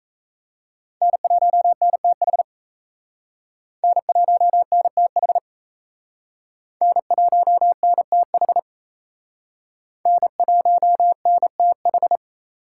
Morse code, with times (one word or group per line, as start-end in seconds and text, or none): 1.0-2.4 N1NTH
3.8-5.4 N1NTH
6.8-8.6 N1NTH
10.1-12.2 N1NTH